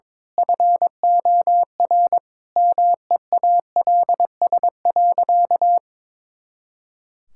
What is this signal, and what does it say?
Morse code, heard as a keyed tone, about 22 wpm, 700 Hz. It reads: FOR MEALS.
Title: Pangram Challenge v2 - 22wpm